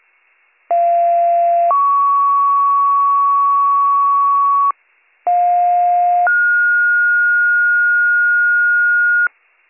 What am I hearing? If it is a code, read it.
Two-tone page, heard as a one-second tone, then a three-second tone, about 690 then 1120 Hz, about 710 then 1430 Hz, with noise. It is heard from 0.7 to 9.3 s.